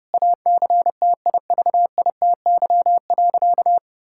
Morse code, heard as a keyed tone, about 32 wpm, 700 Hz. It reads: ACTIVITY.
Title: Sentences from Top 1000 Words - Review Part 6 - 30wpm